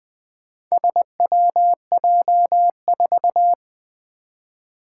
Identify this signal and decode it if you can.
Morse code, heard as a keyed tone, about 21 wpm, 700 Hz. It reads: SWJ4
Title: Four Challenging Letter-Number Combinations 20wpm